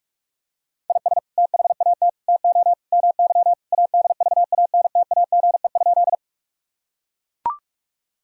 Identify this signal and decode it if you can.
Morse code, heard as a keyed tone, about 41 wpm, 700 Hz. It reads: IS THAT TO MY ADVANTAGE?
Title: Sentences from Top 1000 Words - Review Part 6 - 45wpm